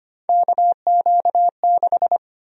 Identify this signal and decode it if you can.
Morse code, heard as a keyed tone, about 25 wpm, 700 Hz. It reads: KQ6
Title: USA Call Sign Prefixes - 25wpm